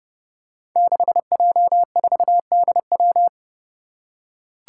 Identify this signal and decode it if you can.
Morse code, heard as a keyed tone, about 29 wpm, 700 Hz. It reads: 6J4DW
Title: Five Challenging Letter-Number Combinations 30wpm